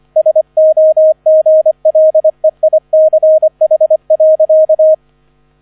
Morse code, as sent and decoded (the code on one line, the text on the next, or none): ... --- --. .-.. .. -.-. .... .-.-.-
SOGLEICH.